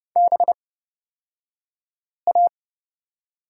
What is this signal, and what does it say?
Morse code, heard as a keyed tone, about 30 wpm, spaced as for 5 wpm, 700 Hz. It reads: BA
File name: Lesson 36d - BT - Two Characters 30wpm